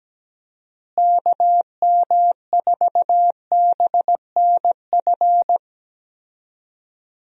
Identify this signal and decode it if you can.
Morse code, heard as a keyed tone, about 17 wpm, 700 Hz. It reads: KM4BNF
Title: USA Call Signs - 6 Characters - Repeated 3x - 17wpm